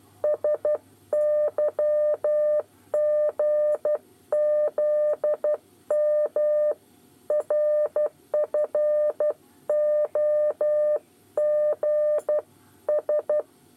Morse code, heard as a keyed tone, about 12 wpm, 600 Hz. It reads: SYGZM RFOG S
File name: 8 bis lezione cw 2